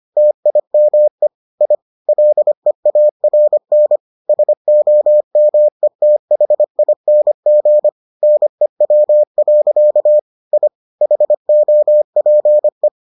Morse code, heard as a keyed tone, about 25 wpm, 600 Hz. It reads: TIME I LEARN SOMETHING NEW. I HOPE